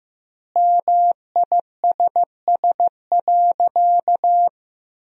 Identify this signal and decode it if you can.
Morse code, heard as a keyed tone, about 15 wpm, 700 Hz. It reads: MISS.
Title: 3x Word-Spacing - Sentences from Top 500 Words - 15wpm